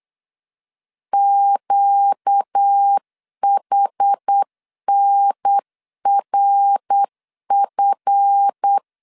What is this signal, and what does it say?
Morse code, heard as a keyed tone, about 8 wpm, 800 Hz. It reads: QHNRF
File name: LEZIONI / 7 GF9Z / 7 GF 8wpm 3108